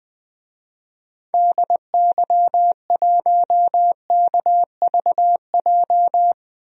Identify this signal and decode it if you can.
Morse code, heard as a keyed tone, about 20 wpm, 700 Hz. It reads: DY1KVJ
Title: DXCC-100 Call Signs - 6 Characters - 20wpm